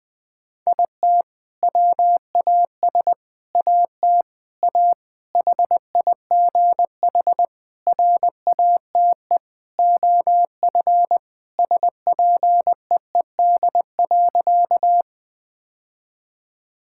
Morse code, heard as a keyed tone, about 20 wpm, 700 Hz. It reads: IT WAS AT A HIGH RATE OF SPEED.
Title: Sentences from Top 1000 Words - Review Part 5 - 20wpm